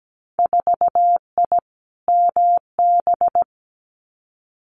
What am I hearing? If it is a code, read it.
Morse code, heard as a keyed tone, about 17 wpm, 700 Hz. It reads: 4I MB
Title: Winter Field Day Exchanges Encore 17wpm